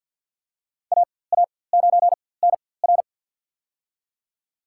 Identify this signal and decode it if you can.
Morse code, heard as a keyed tone, about 48 wpm, spaced as for 23 wpm, 700 Hz. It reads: AA9NR
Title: USA Call Signs - 4x Character Spacing - 50wpm